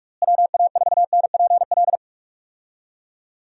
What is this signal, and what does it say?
Morse code, heard as a keyed tone, about 41 wpm, 700 Hz. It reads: WA4NPL
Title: USA Call Signs - 6 Characters - Repeated 3x - 45wpm